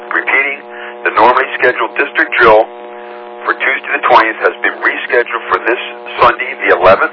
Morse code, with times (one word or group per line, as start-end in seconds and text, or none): none